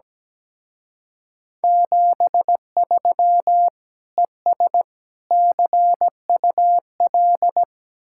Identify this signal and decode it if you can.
Morse code, heard as a keyed tone, about 18 wpm, 700 Hz. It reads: 73 ES CUL
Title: Speed-Racing Common QSO Phrases Encore - 17wpm